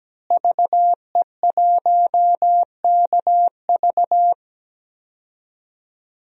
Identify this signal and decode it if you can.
Morse code, heard as a keyed tone, about 17 wpm, 700 Hz. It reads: VE1KV
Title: DXCC-100 Call Signs - 5 Characters - Repeated 2x - 17wpm